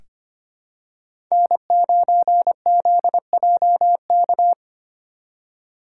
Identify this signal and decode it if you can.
Morse code, heard as a keyed tone, about 25 wpm, 700 Hz. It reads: N9ZJK
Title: USA Call Signs - 5 Characters - 4x Character Spacing - 25wpm